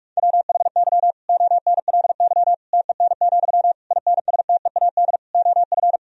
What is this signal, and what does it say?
Morse code, heard as a keyed tone, about 46 wpm, 700 Hz. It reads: WHY ONLY TEN, INSTEAD OF
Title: Sentences from Top 1000 Words - Review Part 4 - 45wpm